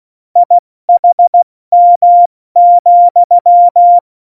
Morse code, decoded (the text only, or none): IHM,